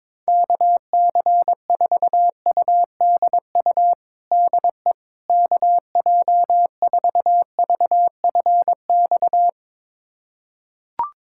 Morse code, heard as a keyed tone, about 22 wpm, 700 Hz. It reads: KC4UDU DE KJ4VFX
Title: USA Callsign DE Callsign - 6 Characters - 22wpm